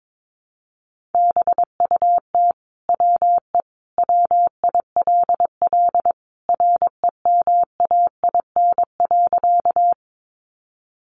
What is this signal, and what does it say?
Morse code, heard as a keyed tone, about 22 wpm, 700 Hz. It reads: BUT WE WILL REMAIN.
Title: Sentences from Top 1200 Words 22wpm